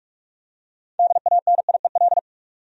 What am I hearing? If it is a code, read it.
Morse code, heard as a keyed tone, about 46 wpm, 700 Hz. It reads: DANIEL
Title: USA Common Names - 45wpm